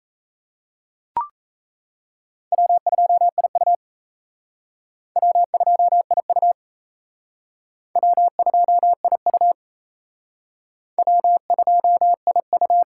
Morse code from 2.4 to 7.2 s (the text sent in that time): W2IU W2IU